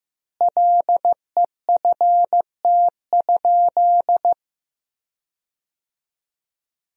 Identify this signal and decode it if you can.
Morse code, heard as a keyed tone, about 15 wpm, 700 Hz. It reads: LEFT?
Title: 4x Word-Spacing - Sentences from Top 300 Words - 15wpm